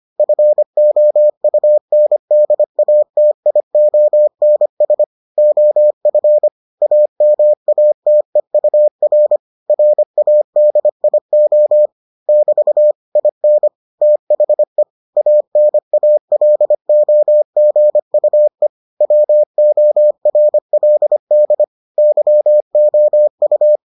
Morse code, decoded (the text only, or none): FOUNDATIONS OF AMATEUR RADIO <BT> IN THE ANALOGUE WORLD YOU